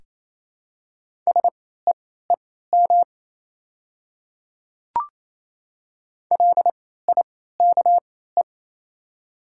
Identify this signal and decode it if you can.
Morse code, heard as a keyed tone, about 28 wpm, spaced as for 16 wpm, 700 Hz. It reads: SEEM LIKE
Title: 3x Character Spacing - Top 300 Words - 28wpm